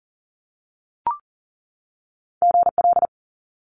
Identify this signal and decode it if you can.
Morse code, heard as a keyed tone, about 38 wpm, 700 Hz. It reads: GL
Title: Multi-letter Phonemes 40wpm